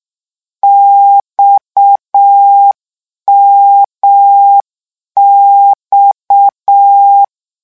Morse code, heard as a keyed tone, about 6 wpm, 800 Hz. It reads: XMX
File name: lecon 1-vitesse 6